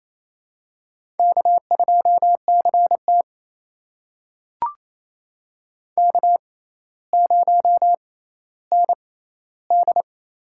Morse code, heard as a keyed tone, about 28 wpm, spaced as for 14 wpm, 700 Hz. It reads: K2CT K0ND
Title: USA Call Signs - 4 Characters - 6x Character Spacing - 28wpm